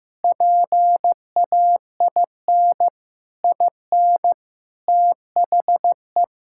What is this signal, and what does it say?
Morse code, heard as a keyed tone, about 15 wpm, 700 Hz. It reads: PAIN IN THE
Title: Sentences from Top 1600 Words 15wpm